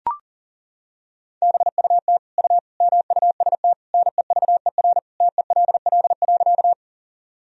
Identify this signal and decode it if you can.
Morse code, heard as a keyed tone, about 38 wpm, 700 Hz. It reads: BUT U MUST NEVER TELL.
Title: Sentences from Top 300 Words and Abbreviations 40wpm